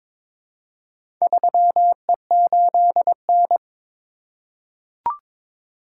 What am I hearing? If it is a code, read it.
Morse code, heard as a keyed tone, about 22 wpm, 700 Hz. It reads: 3E8N
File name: practice / 22wpm / Four Letter-Number Combinations - 22wpm